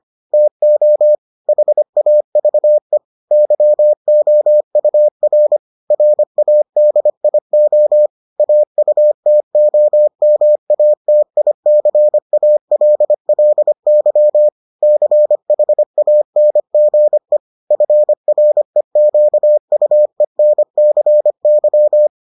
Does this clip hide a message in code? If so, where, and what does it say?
Yes: Morse code from 0.3 to 22.1 s, heard as a keyed tone, about 25 wpm, 600 Hz. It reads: TO HAVE YOUR RADIO AUTOMATICALLY CHANGE FREQUENCY